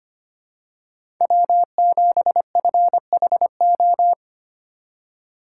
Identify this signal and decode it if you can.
Morse code, heard as a keyed tone, about 25 wpm, 700 Hz. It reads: W7FHO
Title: USA Call Signs - 5 Characters - 2x Character Spacing - 25wpm